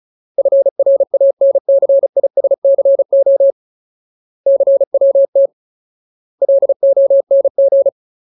Morse code, heard as a keyed tone, about 34 wpm, 550 Hz. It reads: FRANCISCO CWT LONG
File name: Ultimate Word List / 35WPM Ultimate